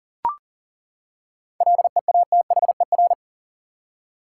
Morse code, heard as a keyed tone, about 40 wpm, 700 Hz. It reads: LEATHER
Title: Seven Letter Words - 40wpm